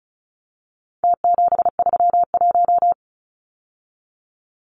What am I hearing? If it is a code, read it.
Morse code, heard as a keyed tone, about 34 wpm, 700 Hz. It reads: T731
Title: Numbers - Contest Style - 4-Digit 35wpm